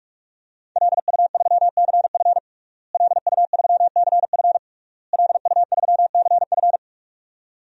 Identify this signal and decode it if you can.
Morse code, heard as a keyed tone, about 46 wpm, 700 Hz. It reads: LU3CF LU3CF LU3CF
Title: DXCC-100 Call Signs - 5 Characters - Repeated 3x - 45wpm